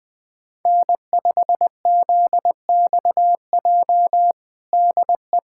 Morse code, heard as a keyed tone, about 20 wpm, 700 Hz. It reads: N5ZXJ DE
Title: USA Callsign DE Callsign - 5 Characters - 20wpm